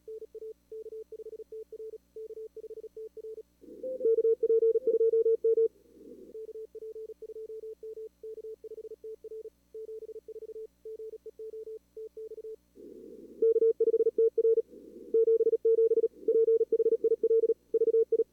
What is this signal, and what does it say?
Morse code, heard as a keyed tone, about 37 wpm, 440 Hz. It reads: NA K5TR K5TR KP2M KP2M K5TR 74 GEO TX K5TR 77 PHIL VI
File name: CWSprint / Feb2022 / k5tr-feb-na-sprint-cw-k5ot -2300